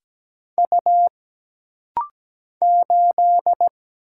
Morse code, heard as a keyed tone, about 17 wpm, 700 Hz. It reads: U 8
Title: Single Letter-Number - Rapid-Fire 17wpm